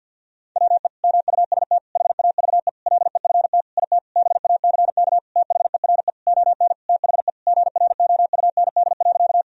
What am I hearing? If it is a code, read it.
Morse code, heard as a keyed tone, about 53 wpm, 700 Hz. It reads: WE MUST HAVE LEFT IT BACK THERE ON THE GROUND.
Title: Sentences from Top 1000 Words - Review Part 2 - 50wpm